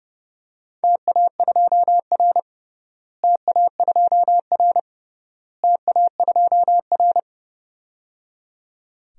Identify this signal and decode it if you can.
Morse code, heard as a keyed tone, about 30 wpm, 700 Hz. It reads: TA2R TA2R TA2R
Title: DXCC-100 Call Signs - 4 Characters - Repeated 3x - 30wpm